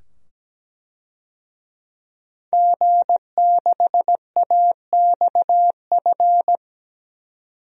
Morse code, heard as a keyed tone, about 17 wpm, 700 Hz. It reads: G6AXF
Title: DXCC-100 Call Signs - 5 Characters - 17wpm